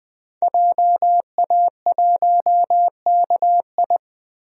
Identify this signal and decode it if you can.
Morse code, heard as a keyed tone, about 20 wpm, 700 Hz. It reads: JA1KI
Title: DXCC-100 Call Signs - 5 Characters - 20wpm